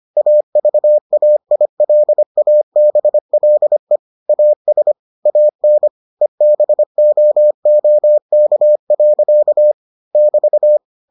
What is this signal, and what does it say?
Morse code, heard as a keyed tone, about 25 wpm, 600 Hz. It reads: AVAILABLE AS AN EBOOK. <BT>